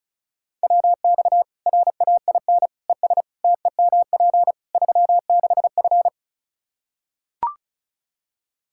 Morse code, heard as a keyed tone, about 34 wpm, 700 Hz. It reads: WX RAIN ES TEMP 36F
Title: Practice QSO - Exchange 2 - Info - 35wpm